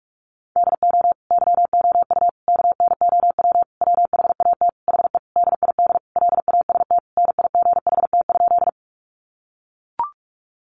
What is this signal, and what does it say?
Morse code, heard as a keyed tone, about 44 wpm, 700 Hz. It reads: DO YOU KNOW WHAT HE DID LAST NIGHT?
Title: Sentences from Top 300 Words 45wpm